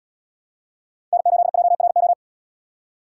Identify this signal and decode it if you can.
Morse code, heard as a keyed tone, about 74 wpm, 700 Hz. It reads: N8JRO